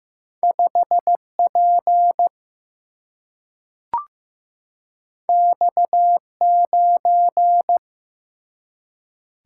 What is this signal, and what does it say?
Morse code, heard as a keyed tone, about 15 wpm, 700 Hz. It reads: 5 P X 9
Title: Single-Letter Single-Number 15wpm